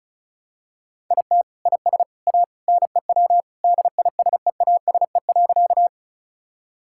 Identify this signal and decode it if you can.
Morse code, heard as a keyed tone, about 34 wpm, 690 Hz. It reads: IT IS A NEW DISEASE.
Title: Sentences from Top 1000 Words - Review Part 6 - 35wpm